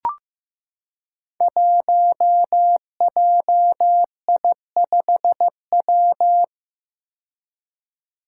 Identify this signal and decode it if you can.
Morse code, heard as a keyed tone, about 15 wpm, 700 Hz. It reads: 1JI5W
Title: Five Challenging Letter-Number Combinations 15wpm